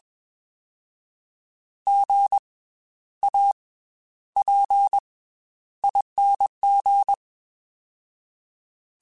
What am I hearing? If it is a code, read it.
Morse code, heard as a keyed tone, about 21 wpm, 800 Hz. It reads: G A P ING
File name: Special INGs / Special INGs-mp3s / Slow ing @20